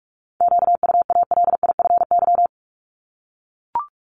Morse code, heard as a keyed tone, about 44 wpm, 700 Hz. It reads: QUALIFY
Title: Top 1751-2000 Words 45wpm